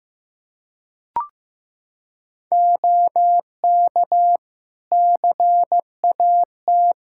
Morse code, heard as a keyed tone, about 16 wpm, 700 Hz. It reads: OK CAT
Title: K1USN SST QSO Practice 15wpm